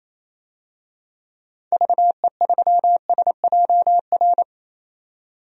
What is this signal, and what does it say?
Morse code, heard as a keyed tone, about 28 wpm, 700 Hz. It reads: VE3SJR
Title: DXCC-100 Call Signs - 6 Characters - 28wpm